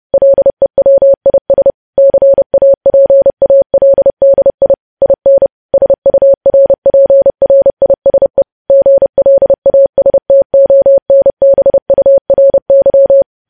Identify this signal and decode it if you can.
Morse code, heard as a keyed tone, about 30 wpm, 550 Hz. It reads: LEWIS CAPALDI IN SURPRISE GLASTONBURY
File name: podcast / News headlines-1x-30x30